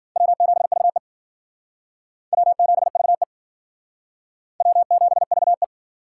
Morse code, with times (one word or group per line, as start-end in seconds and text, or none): 0.2-1.0 W7VE
2.3-3.2 W7VE
4.6-5.6 W7VE